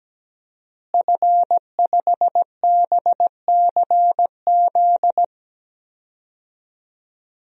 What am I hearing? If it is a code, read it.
Morse code, heard as a keyed tone, about 17 wpm, 700 Hz. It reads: F5BCZ
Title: DXCC-100 Call Signs - 5 Characters - Repeated 3x - 17wpm